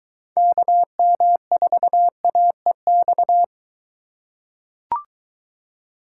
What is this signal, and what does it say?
Morse code, heard as a keyed tone, about 23 wpm, 700 Hz. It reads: KM4AEX